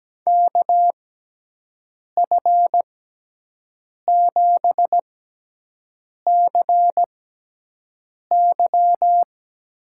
Morse code, heard as a keyed tone, about 17 wpm, spaced as for 6 wpm, 700 Hz. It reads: KF7CY
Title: USA Call Signs - 6x Character Spacing - 17wpm